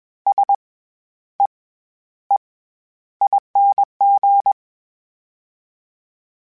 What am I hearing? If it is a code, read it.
Morse code, heard as a keyed tone, about 21 wpm, spaced as for 11 wpm, 800 Hz. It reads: SEEING
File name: Special INGs-mp3s / Special INGs-mp3s / SlowToFaster ing @20